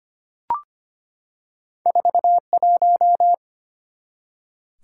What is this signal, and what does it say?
Morse code, heard as a keyed tone, about 25 wpm, 700 Hz. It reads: 41
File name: Two Letter-Number Combinations - 25wpm